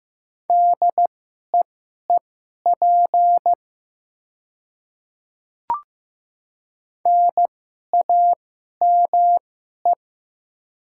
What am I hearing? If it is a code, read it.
Morse code, heard as a keyed tone, about 15 wpm, spaced as for 11 wpm, 690 Hz. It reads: DEEP NAME
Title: 2x Character Spacing - Top 500 Words - 15wpm